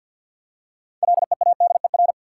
Morse code, heard as a keyed tone, about 51 wpm, 700 Hz. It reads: LEADER
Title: Six Letter Words - 50wpm